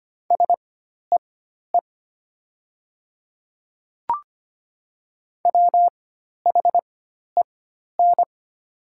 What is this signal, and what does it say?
Morse code, heard as a keyed tone, about 25 wpm, spaced as for 12 wpm, 690 Hz. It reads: SEE WHEN
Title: 4x Character Spacing - Top 200 Words - 25wpm